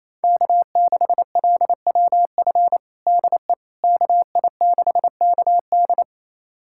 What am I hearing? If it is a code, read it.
Morse code, heard as a keyed tone, about 28 wpm, 700 Hz. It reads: K6LWF DE KI6KD
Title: USA Callsign DE Callsign - 5 Characters - 28wpm